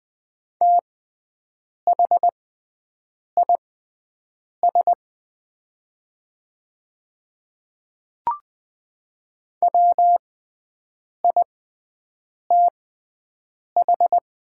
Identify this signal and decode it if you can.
Morse code, heard as a keyed tone, about 20 wpm, spaced as for 7 wpm, 700 Hz. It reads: THIS WITH
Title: 6x Character Spacing - Top 100 Words - 20wpm